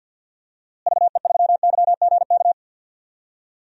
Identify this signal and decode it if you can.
Morse code, heard as a keyed tone, about 51 wpm, 700 Hz. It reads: VE3YGK
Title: DXCC-100 Call Signs - 6 Characters - Repeated 3x - 50wpm